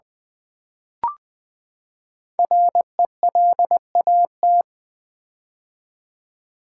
Morse code, heard as a keyed tone, about 21 wpm, 700 Hz. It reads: RELAT
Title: Build A Word - 5 Letters At A Time - 12-Letter Words - 20wpm